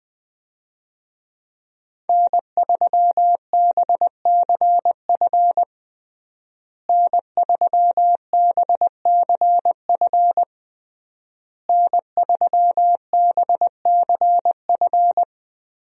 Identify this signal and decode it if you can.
Morse code, heard as a keyed tone, about 21 wpm, 700 Hz. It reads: N3BCF N3BCF N3BCF
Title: USA Call Signs - 5 Characters - Repeated 3x - 20wpm